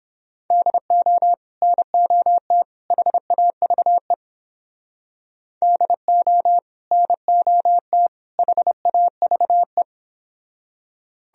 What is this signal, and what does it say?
Morse code, heard as a keyed tone, about 29 wpm, 700 Hz. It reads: DO NOT HAVE DO NOT HAVE